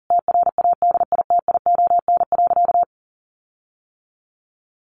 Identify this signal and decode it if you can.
Morse code, heard as a keyed tone, about 40 wpm, 700 Hz. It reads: TRADITION.
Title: Sentences from Top 1500 Words 40wpm